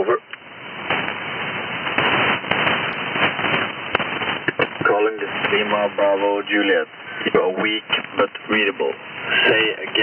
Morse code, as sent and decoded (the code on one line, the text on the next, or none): none